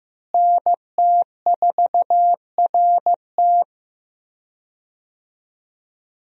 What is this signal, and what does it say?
Morse code, heard as a keyed tone, about 15 wpm, 700 Hz. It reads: NT4RT
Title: USA Call Signs - 5 Characters - Repeated 3x - 15wpm